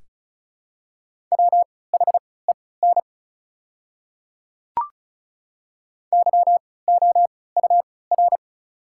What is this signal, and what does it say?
Morse code, heard as a keyed tone, about 34 wpm, spaced as for 20 wpm, 690 Hz. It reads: WHEN YOUR
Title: 3x Character Spacing - Top 100 Words - 35wpm